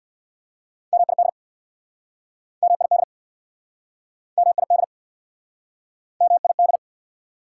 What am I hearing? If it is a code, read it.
Morse code, heard as a keyed tone, about 69 wpm, 690 Hz. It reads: MID MID MID MID